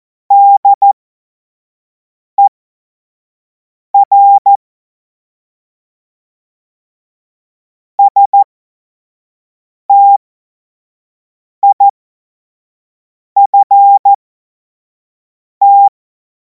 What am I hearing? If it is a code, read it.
Morse code, heard as a keyed tone, about 14 wpm, spaced as for 5 wpm, 800 Hz. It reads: DER STIFT
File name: Morsekurs / Witze / Witz 17. Februar 2026 25 F70